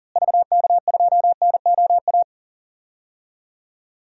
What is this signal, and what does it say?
Morse code, heard as a keyed tone, about 38 wpm, 700 Hz. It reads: VK2NOA
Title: DXCC-100 Call Signs - 6 Characters - Repeated 2x - 40wpm